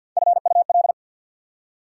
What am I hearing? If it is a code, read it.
Morse code, heard as a keyed tone, about 50 wpm, 690 Hz. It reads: FUL
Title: Prefixes and Suffixes 50wpm